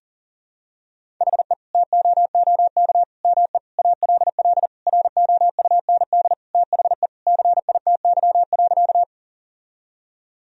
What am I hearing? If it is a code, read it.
Morse code, heard as a keyed tone, about 40 wpm, 700 Hz. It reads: HE TOOK ME ALL ROUND THE CITY.
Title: Sentences from Top 1000 Words - Review Part 1 - 40wpm